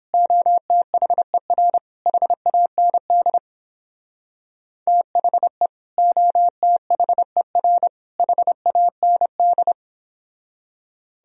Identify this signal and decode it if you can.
Morse code, heard as a keyed tone, about 31 wpm, 700 Hz. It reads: OTHER HAND THE OTHER HAND